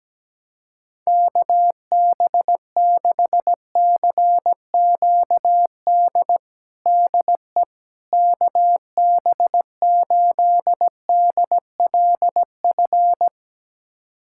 Morse code, heard as a keyed tone, about 17 wpm, 700 Hz. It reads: KB6CQD DE KB8DLF
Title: USA Callsign DE Callsign 17wpm